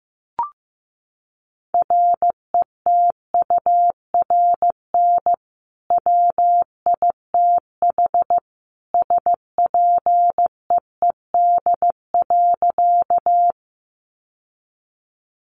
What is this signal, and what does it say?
Morse code, heard as a keyed tone, about 15 wpm, 700 Hz. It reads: RETURN WITH SPEED.
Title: Sentences from Top 800 Words 15wpm